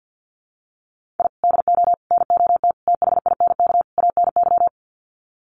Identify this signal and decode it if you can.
Morse code, heard as a keyed tone, about 50 wpm, 700 Hz. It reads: I DO NOT THINK ANY
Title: Sets of 5 Words Encore 50wpm